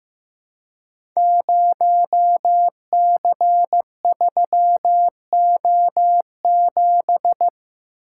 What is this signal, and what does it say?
Morse code, heard as a keyed tone, about 15 wpm, 700 Hz. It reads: 0C3O7
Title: Code Groups - 5 Letter:Numbers - 15wpm